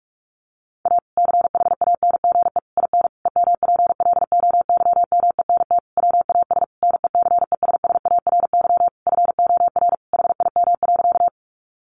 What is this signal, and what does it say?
Morse code, heard as a keyed tone, about 44 wpm, 700 Hz. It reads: A CHANGE IN EMPLOYMENT WAS NECESSARY FOR HIM.
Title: Sentences from Top 1500 Words 45wpm